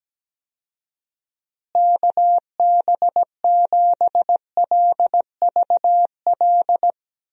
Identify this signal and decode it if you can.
Morse code, heard as a keyed tone, about 17 wpm, 700 Hz. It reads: KB7LVL
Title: USA Call Signs - 6 Characters - Repeated 3x - 17wpm